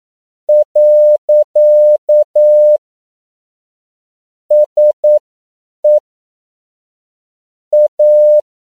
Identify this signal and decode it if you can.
Morse code, heard as a keyed tone, about 9 wpm, spaced as for 7 wpm, 600 Hz. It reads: . SE A
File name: CW / 10PPM